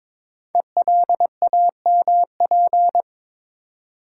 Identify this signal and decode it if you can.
Morse code, heard as a keyed tone, about 23 wpm, 700 Hz. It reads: ELAMP
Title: Code Groups - 5 Letters - 22wpm